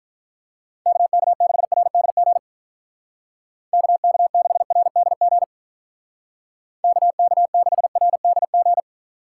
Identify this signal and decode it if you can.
Morse code, heard as a keyed tone, about 43 wpm, 700 Hz. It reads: KK6RDG KK6RDG KK6RDG
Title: Speed Racing - USA 2x3 Call Signs 35wpm